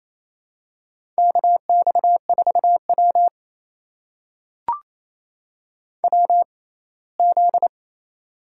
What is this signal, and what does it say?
Morse code, heard as a keyed tone, about 28 wpm, spaced as for 19 wpm, 700 Hz. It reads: KX4W WZ
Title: USA Call Signs - 4 Characters - 6x Character Spacing - 28wpm